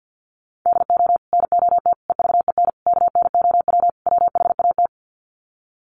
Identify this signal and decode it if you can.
Morse code, heard as a keyed tone, about 50 wpm, 700 Hz. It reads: DO NOT EVEN KNOW WHAT
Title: Sets of 5 Words Encore 50wpm